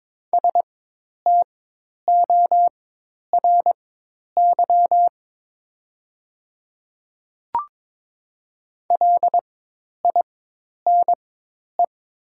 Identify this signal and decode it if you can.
Morse code, heard as a keyed tone, about 22 wpm, spaced as for 10 wpm, 700 Hz. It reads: STORY LINE
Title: 4x Character Spacing - Top 500 Words - 22wpm